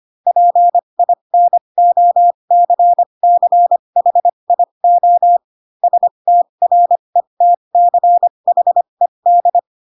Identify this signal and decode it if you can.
Morse code, heard as a keyed tone, about 25 wpm, spaced as for 22 wpm, 700 Hz. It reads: PINOCCHIO STRETCHED
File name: Pinocchio / Pinocchio ch7 18 25 ik1hge